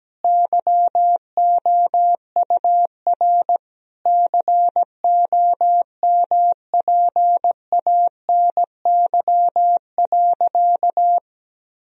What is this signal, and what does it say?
Morse code, heard as a keyed tone, about 17 wpm, 700 Hz. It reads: YOUR COMPANY.
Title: Sentences from Top 1000 Words - Review Part 3 - 17wpm